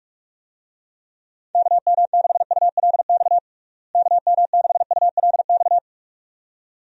Morse code, heard as a keyed tone, about 46 wpm, 700 Hz. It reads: KM6ULX KM6ULX
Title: USA Call Signs - 6 Characters - Repeated 2x - 45wpm